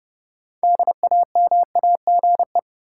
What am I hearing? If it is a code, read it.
Morse code, heard as a keyed tone, about 30 wpm, 700 Hz. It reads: DAMAGE
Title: Sentences from Top 1100 Words - New Word Drill - 30wpm